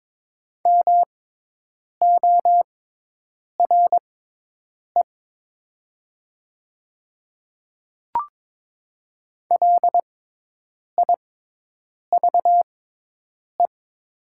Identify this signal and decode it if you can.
Morse code, heard as a keyed tone, about 22 wpm, spaced as for 8 wpm, 700 Hz. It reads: MORE LIVE
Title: 6x Character Spacing - Top 300 Words - 22wpm